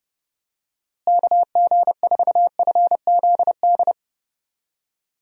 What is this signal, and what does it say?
Morse code, heard as a keyed tone, about 30 wpm, 700 Hz. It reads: KG4FZD
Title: USA Call Signs - 6 Characters - Repeated 3x - 30wpm